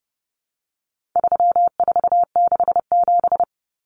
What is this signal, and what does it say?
Morse code, heard as a keyed tone, about 30 wpm, 700 Hz. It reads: 3467
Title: Numbers - 4-Digit 30wpm